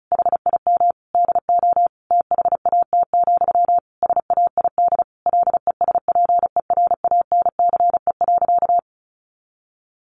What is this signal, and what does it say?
Morse code, heard as a keyed tone, about 34 wpm, 700 Hz. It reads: HIM DO THAT, SAID LESPERANCE.
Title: A Sound Of Thunder 35wpm